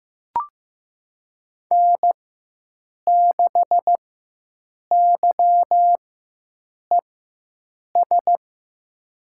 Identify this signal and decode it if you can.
Morse code, heard as a keyed tone, about 15 wpm, spaced as for 7 wpm, 700 Hz. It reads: N6YES
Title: USA Call Signs - 5 Characters - 4x Character Spacing - 15wpm